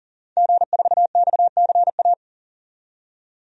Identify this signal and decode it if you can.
Morse code, heard as a keyed tone, about 40 wpm, 700 Hz. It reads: G4XCA
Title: DXCC-100 Call Signs - 5 Characters - 40wpm